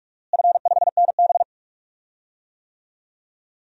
Morse code, heard as a keyed tone, about 46 wpm, 700 Hz. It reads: F5NB
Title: DXCC-100 Call Signs - 4 Characters - Repeated 2x - 45wpm